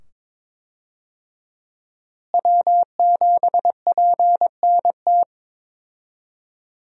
Morse code, heard as a keyed tone, about 22 wpm, 700 Hz. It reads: W7PNT
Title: USA Call Signs - 5 Characters - Repeated 2x - 22wpm